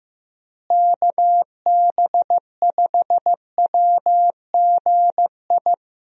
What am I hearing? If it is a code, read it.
Morse code, heard as a keyed tone, about 15 wpm, 700 Hz. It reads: KB5WGI
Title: USA Call Signs - 6 Characters - Repeated 3x - 15wpm